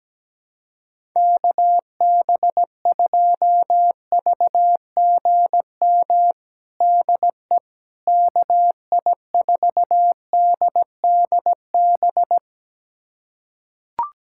Morse code, heard as a keyed tone, about 17 wpm, 700 Hz. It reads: KB2VGM DE KI4DDB
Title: USA Callsign DE Callsign - 6 Characters - 17wpm